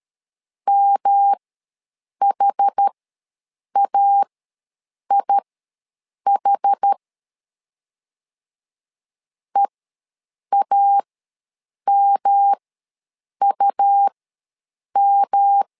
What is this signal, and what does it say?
Morse code, heard as a keyed tone, about 13 wpm, spaced as for 7 wpm, 800 Hz. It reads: MHAIH EAMUM
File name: LEZIONI / 3 MWRO / 3 M ws 12wpm 1745